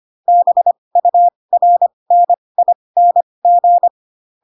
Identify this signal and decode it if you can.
Morse code, heard as a keyed tone, about 25 wpm, spaced as for 20 wpm, 700 Hz. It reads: BURNING
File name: Pinocchio / Pinocchio ch3 15 25 ik1hge